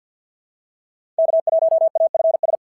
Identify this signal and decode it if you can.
Morse code, heard as a keyed tone, about 48 wpm, 680 Hz. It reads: K1AFS
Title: SOTA QSO Practice 50wpm